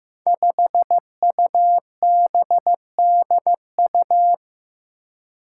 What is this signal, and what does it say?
Morse code, heard as a keyed tone, about 15 wpm, 700 Hz. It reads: 5UBDU
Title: Five Challenging Letter-Number Combinations 15wpm